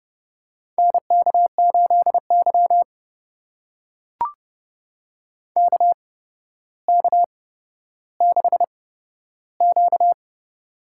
Morse code, heard as a keyed tone, about 30 wpm, spaced as for 13 wpm, 700 Hz. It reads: NK8Y KK6Q
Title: USA Call Signs - 4 Characters - 8x Character Spacing - 30wpm